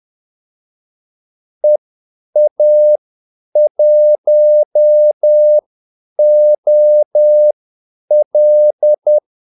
Morse code, heard as a keyed tone, about 10 wpm, spaced as for 8 wpm, 600 Hz. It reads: EA1OL
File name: Callsigns-10-8wpm